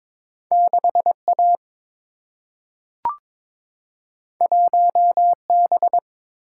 Morse code, heard as a keyed tone, about 22 wpm, 700 Hz. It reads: 6 A 1 B
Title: Single-Letter Single-Number 22wpm